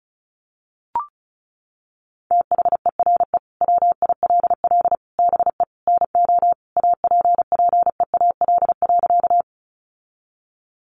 Morse code, heard as a keyed tone, about 34 wpm, 700 Hz. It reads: THERE WILL BE NO APPEAL.
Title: Sentences from Top 1300 Words 35wpm